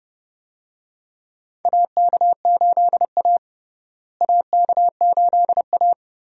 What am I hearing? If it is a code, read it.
Morse code, heard as a keyed tone, about 30 wpm, 700 Hz. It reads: AK8A AK8A
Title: USA Call Signs - 4 Characters - Repeated 3x - 30wpm